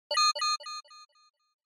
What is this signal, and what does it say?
There is no code.